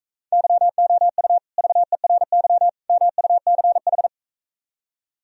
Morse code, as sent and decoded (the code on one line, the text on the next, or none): -.-- --- ..- / ...- . .-. -.-- / -- ..- -.-. ....
YOU VERY MUCH